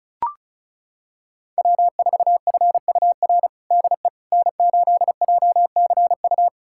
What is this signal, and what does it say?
Morse code, heard as a keyed tone, about 34 wpm, 700 Hz. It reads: W4FUR DE N8JCU
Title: USA Callsign DE Callsign - 5 Characters - 35wpm